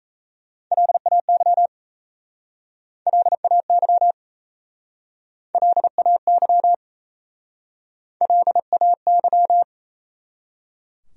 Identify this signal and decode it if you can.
Morse code, heard as a keyed tone, about 38 wpm, 700 Hz. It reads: LAY LAY LAY LAY